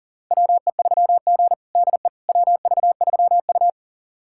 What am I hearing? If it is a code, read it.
Morse code, heard as a keyed tone, about 40 wpm, 700 Hz. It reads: WE3G DE WV3U
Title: USA Callsign DE Callsign - 4 Characters - 40wpm